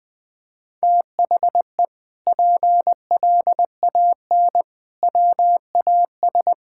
Morse code, heard as a keyed tone, about 20 wpm, 700 Hz. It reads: THE PLAN WAS
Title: Sentences with 8-Word Phrases - Spoken Beginning - 20wpm